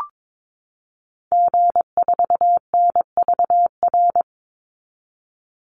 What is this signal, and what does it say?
Morse code, heard as a keyed tone, about 22 wpm, 700 Hz. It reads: G4NVR
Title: Call Signs Repeated 3x - 22wpm